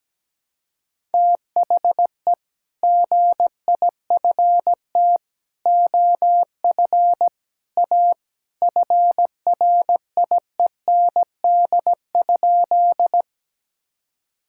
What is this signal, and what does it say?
Morse code, heard as a keyed tone, about 17 wpm, 700 Hz. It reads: THE GIFT OF A FRIEND?
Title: Sentences from Top 1000 Words - Review Part 6 - 17wpm